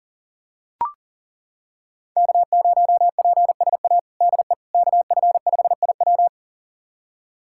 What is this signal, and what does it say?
Morse code, heard as a keyed tone, about 37 wpm, 700 Hz. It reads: K0PSA DE KF5IW
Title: USA Callsign DE Callsign - 5 Characters - 40wpm